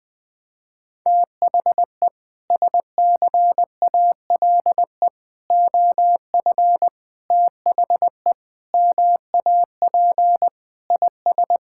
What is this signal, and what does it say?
Morse code, heard as a keyed tone, about 20 wpm, 700 Hz. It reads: THE SCALE OF THE MAP IS
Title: Sentences from Top 1000 Words - Review Part 5 - 20wpm